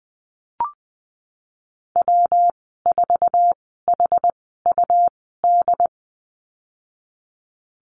Morse code, heard as a keyed tone, about 20 wpm, spaced as for 15 wpm, 700 Hz. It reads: W4HUD
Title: 200 Lesson 32g - Letter k - Call Signs 20wpm